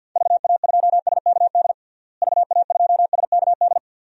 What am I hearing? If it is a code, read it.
Morse code, heard as a keyed tone, about 50 wpm, 700 Hz. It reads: VA2SKD VA2SKD
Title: DXCC-100 Call Signs - 6 Characters - Repeated 2x - 50wpm